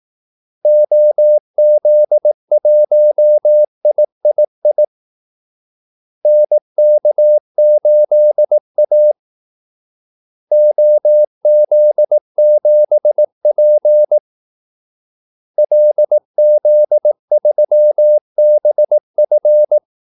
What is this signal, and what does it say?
Morse code, heard as a keyed tone, about 18 wpm, 600 Hz. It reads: OZ1III NK8A OZ7P LZ3BF
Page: Kaldesignaler 18 wpm